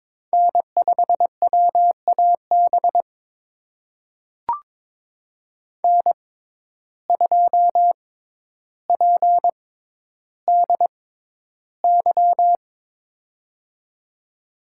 Morse code, heard as a keyed tone, about 22 wpm, spaced as for 11 wpm, 700 Hz. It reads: N5WAB N2PDY
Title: USA Call Signs - 5 Characters - 6x Character Spacing - 22wpm